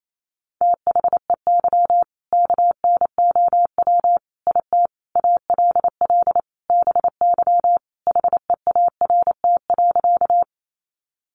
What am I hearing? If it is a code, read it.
Morse code, heard as a keyed tone, about 28 wpm, 700 Hz. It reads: THEY KNOW IT ALL BY HEART.
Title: Sentences from Top 1200 Words 28wpm